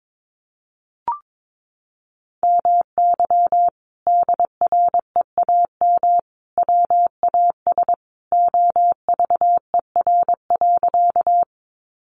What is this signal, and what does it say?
Morse code, heard as a keyed tone, about 22 wpm, 700 Hz. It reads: MY DREAM WAS OVER.
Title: Sentences from Top 1500 Words 22wpm